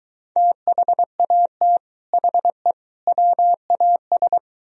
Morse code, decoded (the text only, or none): THAT HE WAS